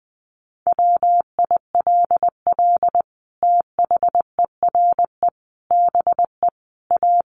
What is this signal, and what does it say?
Morse code, heard as a keyed tone, about 20 wpm, 700 Hz. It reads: WILL THERE BE A
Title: Sentences from Top 1300 Words 20wpm